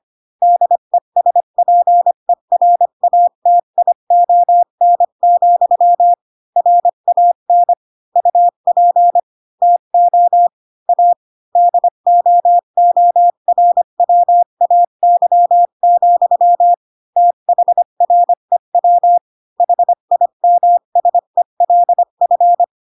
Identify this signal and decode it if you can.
Morse code, heard as a keyed tone, about 25 wpm, 700 Hz. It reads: DESPERATION, RAN UP TO A DOORWAY, THREW HIMSELF